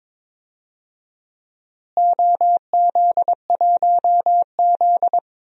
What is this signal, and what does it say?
Morse code, heard as a keyed tone, about 22 wpm, 700 Hz. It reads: OZ1Z
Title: DXCC-100 Call Signs - 4 Characters - Repeated 2x - 22wpm